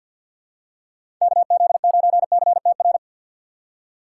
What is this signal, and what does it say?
Morse code, heard as a keyed tone, about 52 wpm, 700 Hz. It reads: KZ9CTR
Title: USA Call Signs - 4x Character Spacing - 50wpm